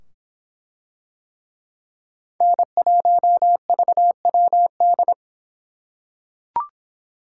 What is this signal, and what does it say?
Morse code, heard as a keyed tone, about 24 wpm, 700 Hz. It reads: N1VWD